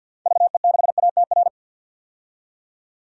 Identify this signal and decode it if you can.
Morse code, heard as a keyed tone, about 50 wpm, 700 Hz. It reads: VE6ATR
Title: DXCC-100 Call Signs - 6 Characters - 50wpm